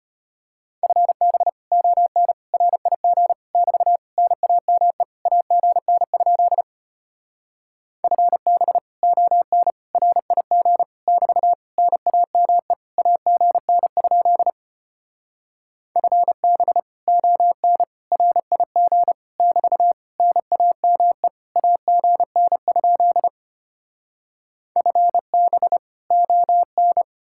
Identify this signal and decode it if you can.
Morse code, heard as a keyed tone, about 33 wpm, 700 Hz. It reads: FB ON RIG <BT> NAME AGN? FB ON RIG <BT> NAME AGN? FB ON RIG <BT> NAME AGN? FB ON